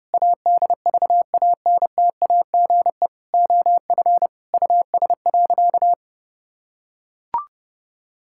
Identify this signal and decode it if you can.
Morse code, heard as a keyed tone, about 32 wpm, 700 Hz. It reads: ADVANTAGE OF US.
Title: Sentences from Top 1000 Words - Review Part 6 - 30wpm